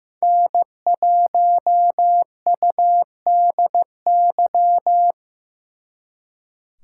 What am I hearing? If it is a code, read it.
Morse code, heard as a keyed tone, about 18 wpm, 700 Hz. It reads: N1UDY
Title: Speed Racing - USA 1x3 Call Signs 15wpm